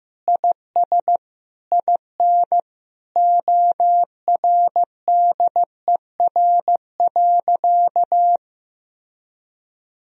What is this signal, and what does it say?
Morse code, heard as a keyed tone, about 15 wpm, 700 Hz. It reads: IS IN ORDER.
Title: Sentences from Top 1000 Words - Review Part 2 - 15wpm